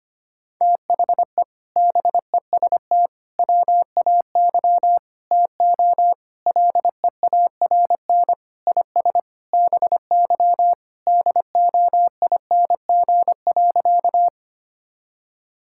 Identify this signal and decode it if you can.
Morse code, heard as a keyed tone, about 25 wpm, 700 Hz. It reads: THE BEST WAY TO LEARN IS BY DOING.
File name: practice / 25wpm / Sentences with 4-Word Phrases - 25wpm